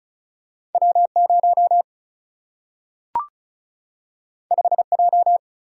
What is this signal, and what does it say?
Morse code, heard as a keyed tone, about 36 wpm, 700 Hz. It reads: W 0 5 J
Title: Single-Letter Single-Number 35wpm